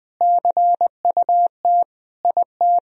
Morse code, heard as a keyed tone, about 20 wpm, 700 Hz. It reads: CUT IT
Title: Sentences from Top 1000 Words - Review Part 1 - 20wpm